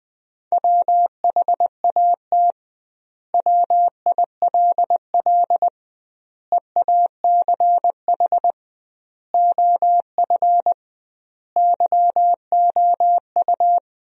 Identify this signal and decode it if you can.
Morse code, heard as a keyed tone, about 20 wpm, 700 Hz. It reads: WHAT WILL EACH OF YOU
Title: 2x Word-Spacing - Sentences from Top 200 Words - 20wpm